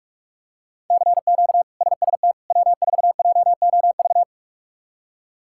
Morse code, heard as a keyed tone, about 46 wpm, 700 Hz. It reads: CQ SST W4JOV
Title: K1USN SST QSO Practice - 45wpm